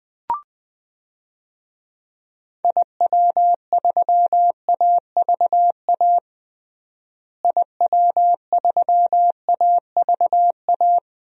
Morse code, heard as a keyed tone, about 20 wpm, 700 Hz. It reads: IW3AVA IW3AVA
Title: DXCC-100 Call Signs - 6 Characters - Repeated 2x - 20wpm